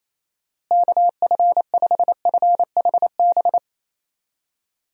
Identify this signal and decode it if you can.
Morse code, heard as a keyed tone, about 28 wpm, 700 Hz. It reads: KF5FHB
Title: USA Call Signs - 6 Characters - Repeated 3x - 28wpm